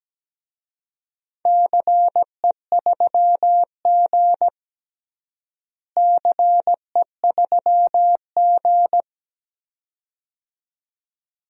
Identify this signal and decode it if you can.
Morse code, heard as a keyed tone, about 17 wpm, 700 Hz. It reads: CE3G CE3G
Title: DXCC-100 Call Signs - 4 Characters - Repeated 2x - 17wpm